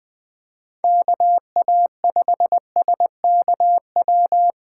Morse code, heard as a keyed tone, about 21 wpm, 700 Hz. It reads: KA5SKW
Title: USA Call Signs - 6 Characters - 8x Character Spacing - 20wpm